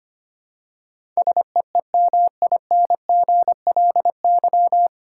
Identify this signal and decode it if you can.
Morse code, heard as a keyed tone, about 25 wpm, 700 Hz. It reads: SEEMINGLY
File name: Nine Letter Words - 25wpm